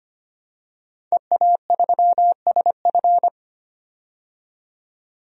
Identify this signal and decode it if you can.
Morse code, heard as a keyed tone, about 25 wpm, 700 Hz. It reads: EA3SF
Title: DXCC-100 Call Signs - 5 Characters - 25wpm